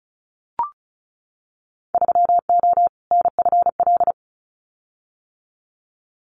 Morse code, heard as a keyed tone, about 34 wpm, 700 Hz. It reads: 3O NFL
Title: Winter Field Day Exchanges 35wpm